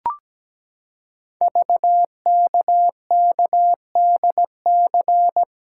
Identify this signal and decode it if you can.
Morse code, heard as a keyed tone, about 17 wpm, 700 Hz. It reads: VKKDC
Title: Code Groups - 5 Letters - 17wpm